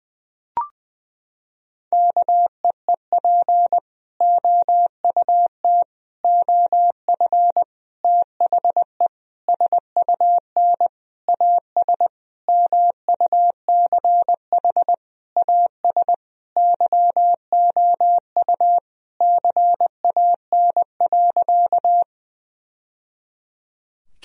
Morse code, heard as a keyed tone, about 21 wpm, 700 Hz. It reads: KEEP OUT OF THE SUN AS MUCH AS YOU CAN.
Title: Sentences from Top 1000 Words - Review Part 1 - 20wpm